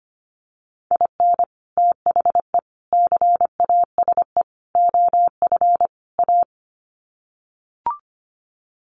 Morse code, heard as a keyed tone, about 25 wpm, 700 Hz. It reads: IN THE CASE OF A
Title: Sets of 5 Words Encore 25wpm